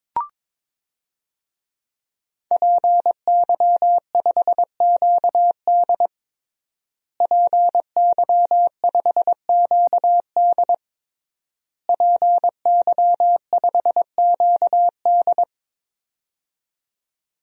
Morse code, heard as a keyed tone, about 22 wpm, 700 Hz. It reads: PY5QD PY5QD PY5QD
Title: DXCC-100 Call Signs - 5 Characters - Repeated 3x - 22wpm